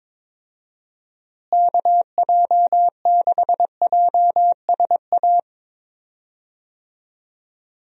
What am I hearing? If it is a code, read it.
Morse code, heard as a keyed tone, about 23 wpm, 700 Hz. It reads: KJ6JSA
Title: USA Call Signs - 6 Characters - Repeated 3x - 22wpm